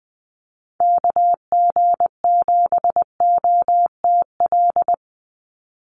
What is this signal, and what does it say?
Morse code, heard as a keyed tone, about 20 wpm, 700 Hz. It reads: KG7OTL
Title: USA Call Signs 20wpm